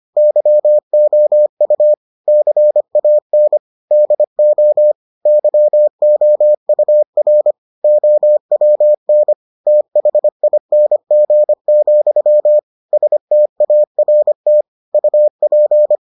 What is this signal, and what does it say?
Morse code, heard as a keyed tone, about 25 wpm, 600 Hz. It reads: YOU CAN DO YOUR OWN THING, START UP